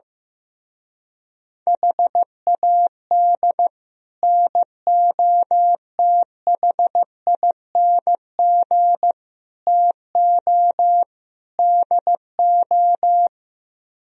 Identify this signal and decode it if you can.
Morse code, heard as a keyed tone, about 16 wpm, 700 Hz. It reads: HAD NOTHING TO DO
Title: Speed-Racing - Sets of 4 Words 15wpm